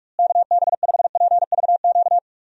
Morse code, heard as a keyed tone, about 46 wpm, 700 Hz. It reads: KB5PVQ
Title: USA Call Signs - 6 Characters - Repeated 3x - 45wpm